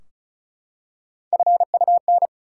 Morse code, heard as a keyed tone, about 34 wpm, 700 Hz. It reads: FUN
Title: Three Letter Words - 35wpm